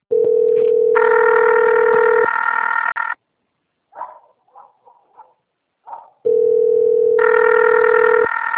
Ringback tone (North America), 440 + 480 Hz.
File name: out- - - - - .14584